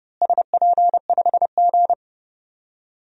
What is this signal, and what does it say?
Morse code, heard as a keyed tone, about 30 wpm, 700 Hz. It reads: SP5G
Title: DXCC-100 Call Signs - 4 Characters - Repeated 3x - 30wpm